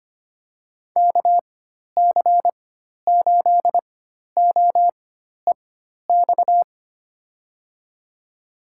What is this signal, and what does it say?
Morse code, heard as a keyed tone, about 25 wpm, spaced as for 12 wpm, 700 Hz. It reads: KC8OEX
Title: USA Call Signs - 6 Characters - 4x Character Spacing - 25wpm